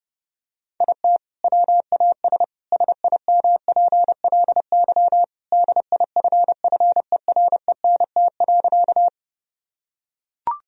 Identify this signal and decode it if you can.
Morse code, heard as a keyed tone, about 29 wpm, 700 Hz. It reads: IT WAS SIMPLY DIFFERENT.
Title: Sentences from Top 1000 Words - Review Part 4 - 30wpm